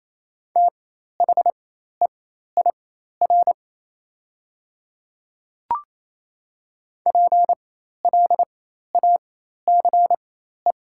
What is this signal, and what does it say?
Morse code, heard as a keyed tone, about 28 wpm, spaced as for 13 wpm, 690 Hz. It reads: THEIR PLACE
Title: 4x Character Spacing - Top 200 Words - 28wpm